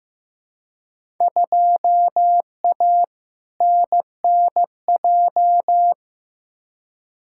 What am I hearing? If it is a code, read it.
Morse code, heard as a keyed tone, about 15 wpm, 700 Hz. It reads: 2A NNJ
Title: Field Day Exchanges - 15wpm